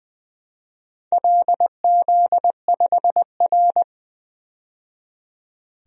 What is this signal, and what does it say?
Morse code, heard as a keyed tone, about 21 wpm, 700 Hz. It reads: LZ5R
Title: DXCC-100 Call Signs - 4 Characters - 20wpm